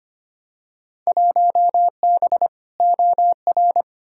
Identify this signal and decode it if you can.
Morse code, heard as a keyed tone, about 25 wpm, 700 Hz. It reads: 1B OR
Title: Field Day Exchanges Encore - 25wpm